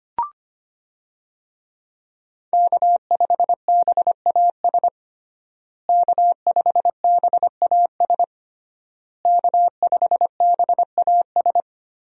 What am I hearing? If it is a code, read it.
Morse code, heard as a keyed tone, about 25 wpm, 700 Hz. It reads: K5BAS K5BAS K5BAS
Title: USA Call Signs - 5 Characters - Repeated 3x - 25wpm